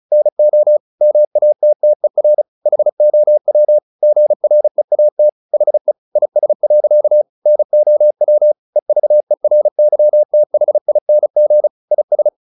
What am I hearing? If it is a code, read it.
Morse code, heard as a keyed tone, about 35 wpm, 600 Hz. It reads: NO MATTER HOW GREAT HE IS. NOW EVERYTHING IS
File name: the old man and the sea / the old man and the sea 1 35wpm